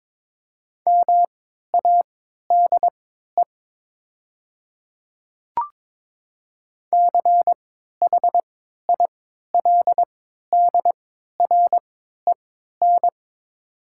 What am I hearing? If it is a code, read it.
Morse code, heard as a keyed tone, about 22 wpm, spaced as for 13 wpm, 700 Hz. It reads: MADE CHILDREN
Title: 3x Character Spacing - Top 500 Words - 22wpm